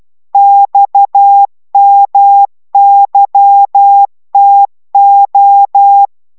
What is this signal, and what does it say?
Morse code, heard as a keyed tone, about 12 wpm, 800 Hz. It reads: XMYTO
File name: taviro tanulas / cw 69 19 10